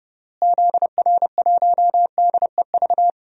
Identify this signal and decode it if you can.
Morse code, heard as a keyed tone, about 30 wpm, 700 Hz. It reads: ZR1DEV
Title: DXCC-100 Call Signs - 6 Characters - Repeated 3x - 30wpm